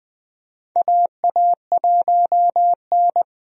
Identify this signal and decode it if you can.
Morse code, heard as a keyed tone, about 20 wpm, 700 Hz. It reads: AA1N
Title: USA Call Signs - 4 Characters - Repeated 3x - 20wpm